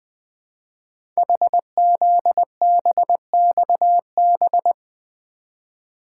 Morse code, heard as a keyed tone, about 20 wpm, 700 Hz. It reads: HZBXB
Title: Code Groups - 5 Letters - 20wpm